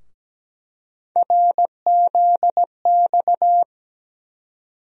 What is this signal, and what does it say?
Morse code, heard as a keyed tone, about 17 wpm, 700 Hz. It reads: RZX
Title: USA Call Sign Postfixes - 3 Letters 17wpm